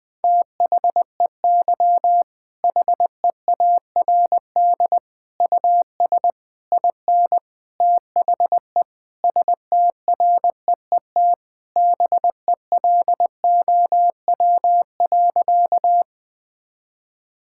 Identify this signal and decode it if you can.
Morse code, heard as a keyed tone, about 20 wpm, 700 Hz. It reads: THEY HEARD US IN THE STREET BELOW.
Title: Sentences from Top 1000 Words - Review Part 5 - 20wpm